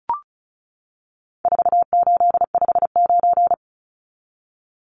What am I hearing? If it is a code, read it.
Morse code, heard as a keyed tone, about 34 wpm, 700 Hz. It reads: 4859
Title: Numbers - 4-Digit 35wpm